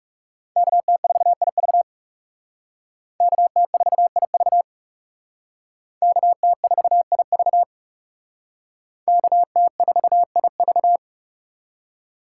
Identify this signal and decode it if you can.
Morse code, heard as a keyed tone, about 37 wpm, 700 Hz. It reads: KT4IV KT4IV KT4IV KT4IV
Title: Speed Racing - USA 2x2 Call Signs 30wpm